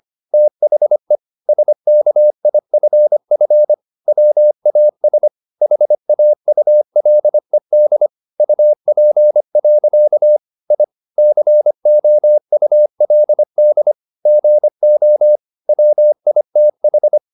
Morse code, heard as a keyed tone, about 25 wpm, 600 Hz. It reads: THE SKIFF WAS HAULED UP. I COULD GO WITH
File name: the old man and the sea part 1 25wpm